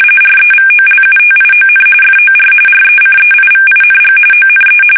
RTTY, about 45.45 baud. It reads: REVOLUTION FOR RECEIVING ONE
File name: fsk850